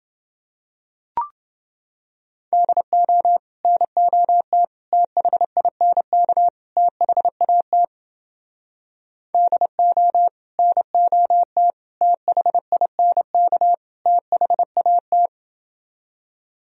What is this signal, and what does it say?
Morse code, heard as a keyed tone, about 30 wpm, 700 Hz. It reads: DO NOT THINK THAT DO NOT THINK THAT